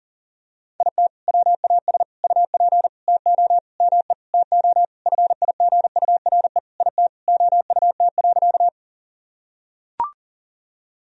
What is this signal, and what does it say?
Morse code, heard as a keyed tone, about 40 wpm, 700 Hz. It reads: IT WAS UP TO ME TO FIGURE IT OUT.
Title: Sentences from Top 1000 Words - Review Part 2 - 40wpm